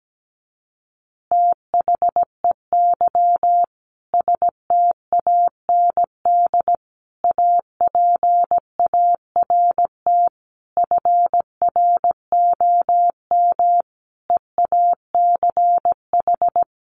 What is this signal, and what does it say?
Morse code, heard as a keyed tone, about 17 wpm, 700 Hz. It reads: THEY STAND APART FROM EACH
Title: Easy Phrases 17wpm